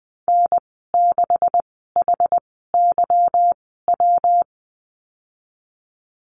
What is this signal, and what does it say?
Morse code, heard as a keyed tone, about 20 wpm, spaced as for 15 wpm, 700 Hz. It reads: N6HYW
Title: 193 Lesson 31g - Letter v - Call Signs 20wpm